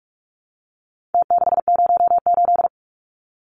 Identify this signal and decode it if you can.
Morse code, heard as a keyed tone, about 44 wpm, 700 Hz. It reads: T608
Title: Numbers - Contest Style - 4-Digit 45wpm